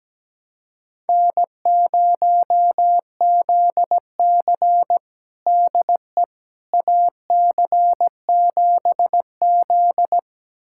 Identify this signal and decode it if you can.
Morse code, heard as a keyed tone, about 17 wpm, 700 Hz. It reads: N0ZC DE AC7Z
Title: USA Callsign DE Callsign - 4 Characters - 17wpm